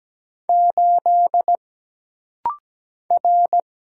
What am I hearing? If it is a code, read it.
Morse code, heard as a keyed tone, about 17 wpm, 700 Hz. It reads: 8 R
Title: Single Letter-Number Alt - Rapid-Fire 17wpm